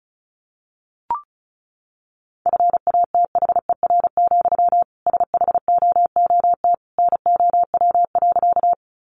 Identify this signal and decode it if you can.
Morse code, heard as a keyed tone, about 34 wpm, 700 Hz. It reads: FATHER, SHOOT NOW.
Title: Sentences from Top 1200 Words 35wpm